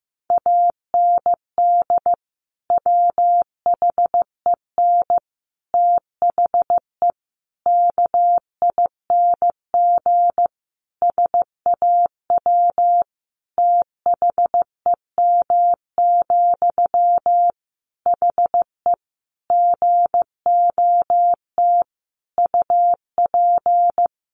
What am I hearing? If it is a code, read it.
Morse code, heard as a keyed tone, about 15 wpm, 700 Hz. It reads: AND WHEN THE KING SAW THEM, HE GOT UP
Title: Sentences from Top 1100 Words 15wpm